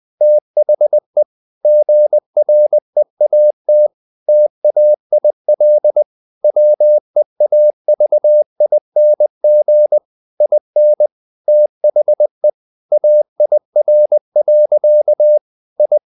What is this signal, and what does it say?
Morse code, heard as a keyed tone, about 20 wpm, 600 Hz. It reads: THE GREAT TAIL WEAVING IN THE AIR. I
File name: the old man and the sea part 2 20wpm